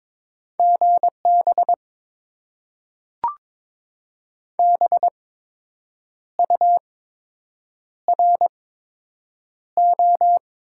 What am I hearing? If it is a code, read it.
Morse code, heard as a keyed tone, about 22 wpm, spaced as for 7 wpm, 700 Hz. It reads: GB BURO
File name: Top 90 QSO Elements - 8x Character Spacing - 22wpm